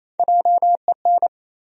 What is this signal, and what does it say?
Morse code, heard as a keyed tone, about 28 wpm, 700 Hz. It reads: JEN
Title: USA Common Names - 28wpm